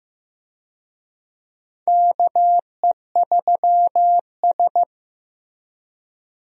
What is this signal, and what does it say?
Morse code, heard as a keyed tone, about 15 wpm, 700 Hz. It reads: KE3S
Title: USA Call Signs - 4 Characters - Repeated 2x - 15wpm